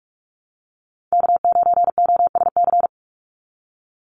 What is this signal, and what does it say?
Morse code, heard as a keyed tone, about 44 wpm, 700 Hz. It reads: K9OSC
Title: NA SOTA Chaser Callsigns 45wpm